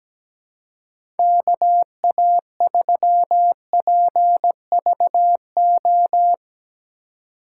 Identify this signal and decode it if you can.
Morse code, heard as a keyed tone, about 17 wpm, 700 Hz. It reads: KA3PVO
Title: USA Call Signs - 6 Characters - 4x Character Spacing - 17wpm